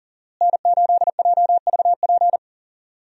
Morse code, heard as a keyed tone, about 37 wpm, 700 Hz. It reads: N8JVP
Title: USA Call Signs - 5 Characters - Repeated 2x - 40wpm